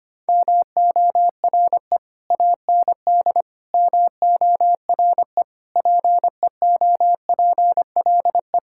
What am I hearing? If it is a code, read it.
Morse code, heard as a keyed tone, about 25 wpm, 700 Hz. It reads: MORE AND MORE PEOPLE
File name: practice / 25wpm / Sentences with 3-Word Phrases - 25wpm